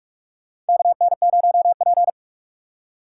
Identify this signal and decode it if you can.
Morse code, heard as a keyed tone, about 45 wpm, 700 Hz. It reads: KN0P
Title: USA Call Signs - 4 Characters - 3x Character Spacing - 45wpm